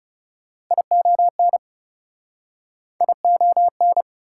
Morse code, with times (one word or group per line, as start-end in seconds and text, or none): 0.7-1.6 ION
3.0-4.0 ION